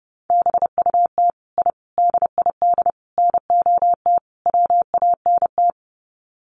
Morse code, heard as a keyed tone, about 29 wpm, 700 Hz. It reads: BUT I DID NOT WANT
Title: Sets of 5 Words Encore 30wpm